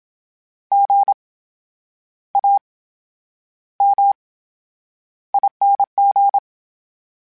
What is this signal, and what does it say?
Morse code, heard as a keyed tone, about 27 wpm, spaced as for 9 wpm, 800 Hz. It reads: GAMING
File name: Special INGs / Special INGs-mp3s / VerySlowToFaster Ing @25